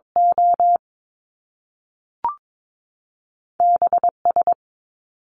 Morse code, heard as a keyed tone, about 22 wpm, spaced as for 3 wpm, 700 Hz. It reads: OB S
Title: Top Two Letter Combinations in Words 22wpm